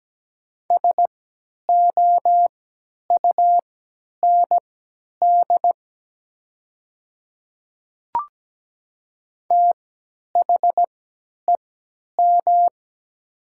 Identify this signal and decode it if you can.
Morse code, heard as a keyed tone, about 17 wpm, spaced as for 10 wpm, 700 Hz. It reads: SOUND THEM
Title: 3x Character Spacing - Top 100 Words - 17wpm